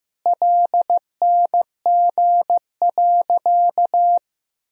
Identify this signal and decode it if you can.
Morse code, heard as a keyed tone, about 15 wpm, 700 Hz. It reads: LNG.
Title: Sentences from Top 500 Words and Abbreviations 15wpm